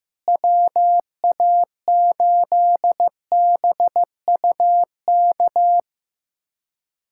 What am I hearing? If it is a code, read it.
Morse code, heard as a keyed tone, about 15 wpm, 700 Hz. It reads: WA8BUK
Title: USA Callsign DE Callsign - 6 Characters - 15wpm